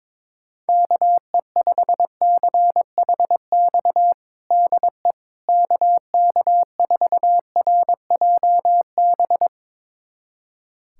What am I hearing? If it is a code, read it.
Morse code, heard as a keyed tone, about 22 wpm, 700 Hz. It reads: KE5CHX DE KK4RJB
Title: USA Callsign DE Callsign - 6 Characters - 22wpm